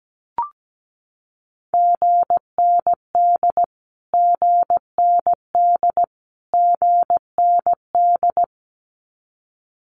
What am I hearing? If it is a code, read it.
Morse code, heard as a keyed tone, about 17 wpm, 700 Hz. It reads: GND GND GND
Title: Abbreviations Repeated 3x - 17wpm